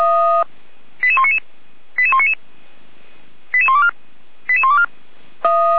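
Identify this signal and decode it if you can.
Five-tone selective call (ZVEI), heard as quick runs of five tones, tones of about 70 ms. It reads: 88199 88199 88124 88124
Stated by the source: ZVEI five-tone call 88129 received by SDR, recorded by Feuerwehr Orlamünde.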